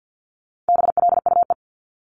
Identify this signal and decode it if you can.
Morse code, heard as a keyed tone, about 50 wpm, 700 Hz. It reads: BLUE
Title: Multi-letter Phonemes 50wpm